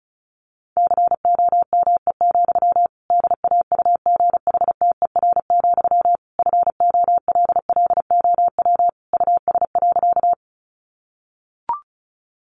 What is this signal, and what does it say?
Morse code, heard as a keyed tone, about 34 wpm, 700 Hz. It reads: COME, DAUGHTER, FOLLOW US.
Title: Sentences from Top 1500 Words 35wpm